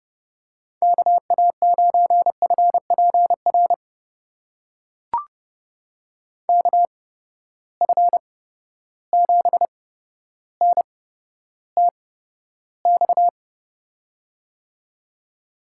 Morse code, heard as a keyed tone, about 31 wpm, spaced as for 13 wpm, 700 Hz. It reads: KA9FPR KF7NTX
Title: USA Call Signs - 6 Characters - 8x Character Spacing - 30wpm